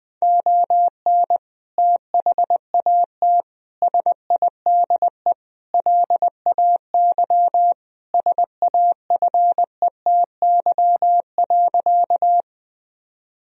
Morse code, heard as a keyed tone, about 20 wpm, 700 Hz. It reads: ON THAT SIDE LAY SAFETY.
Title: Sentences from Top 1000 Words - Review Part 4 - 20wpm